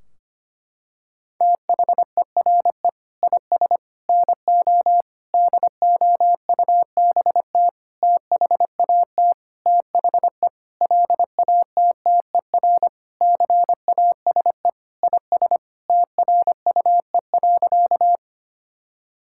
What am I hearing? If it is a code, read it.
Morse code, heard as a keyed tone, about 25 wpm, 700 Hz. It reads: THERE IS NO DOUBT THAT THE LATTER CASE IS TRUE.
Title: Sentences from Top 1600 Words 25wpm